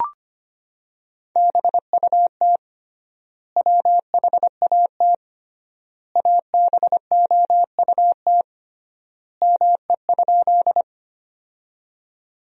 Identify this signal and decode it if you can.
Morse code, heard as a keyed tone, about 25 wpm, 700 Hz. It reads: BUT WHAT ABOUT ME?
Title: 3x Word-Spacing - Sentences from Top 200 Words - 25wpm